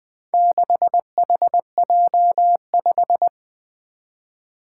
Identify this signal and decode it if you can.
Morse code, heard as a keyed tone, about 20 wpm, 700 Hz. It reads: 6HJ5
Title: Four Challenging Letter-Number Combinations 20wpm